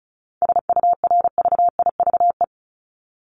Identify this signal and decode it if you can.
Morse code, heard as a keyed tone, about 34 wpm, 700 Hz. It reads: SURVIVE
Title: Top 1251-1500 Words 35wpm